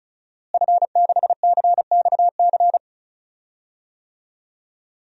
Morse code, heard as a keyed tone, about 36 wpm, 700 Hz. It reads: F6CXC
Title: DXCC-100 Call Signs - 5 Characters - Repeated 2x - 35wpm